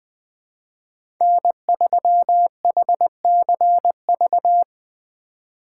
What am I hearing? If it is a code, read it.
Morse code, heard as a keyed tone, about 20 wpm, 700 Hz. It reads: N3HCV
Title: USA Call Signs - 5 Characters - 3x Character Spacing - 20wpm